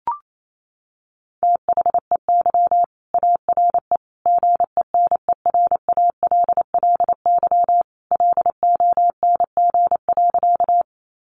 Morse code, heard as a keyed tone, about 28 wpm, 700 Hz. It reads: THEY ARE GENERALLY LONG.
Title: Sentences from Top 600 Words 28wpm